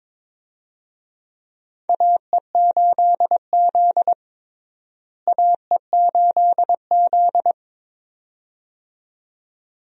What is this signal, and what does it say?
Morse code, heard as a keyed tone, about 22 wpm, 700 Hz. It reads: AE8Z AE8Z
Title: USA Call Signs - 4 Characters - Repeated 2x - 22wpm